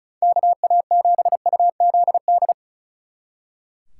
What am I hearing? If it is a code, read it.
Morse code, heard as a keyed tone, about 33 wpm, 700 Hz. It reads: KA7UZD
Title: Speed Racing - USA 2x3 Call Signs 35wpm